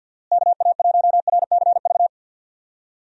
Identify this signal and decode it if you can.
Morse code, heard as a keyed tone, about 51 wpm, 700 Hz. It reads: KA1RCV
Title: USA Call Signs - 6 Characters - Repeated 2x - 50wpm